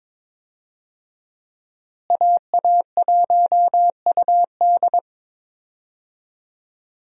Morse code, heard as a keyed tone, about 22 wpm, 700 Hz. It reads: AA1UD
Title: USA Call Signs - 5 Characters - Repeated 2x - 22wpm